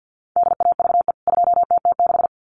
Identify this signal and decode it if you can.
Morse code, heard as a keyed tone, about 50 wpm, 700 Hz. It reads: DAVE 2TT6
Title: CWT Exchange 50wpm